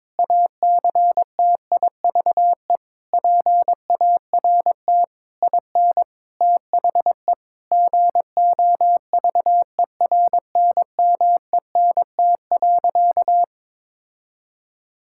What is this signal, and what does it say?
Morse code, heard as a keyed tone, about 22 wpm, 700 Hz. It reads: ACTIVE PART IN THE GOVERNMENT.
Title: Sentences from Top 1000 Words - Review Part 6 - 22wpm